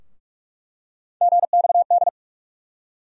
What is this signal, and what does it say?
Morse code, heard as a keyed tone, about 45 wpm, 700 Hz. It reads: GXD
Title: USA Call Sign Postfixes - 3 Letters 45wpm